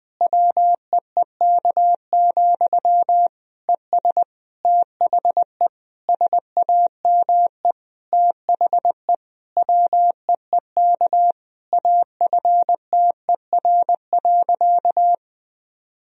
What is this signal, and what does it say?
Morse code, heard as a keyed tone, about 20 wpm, 700 Hz. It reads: WEEK, ES THE SAME THE WEEK AFTER.
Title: Sentences from Top 500 Words and Abbreviations 20wpm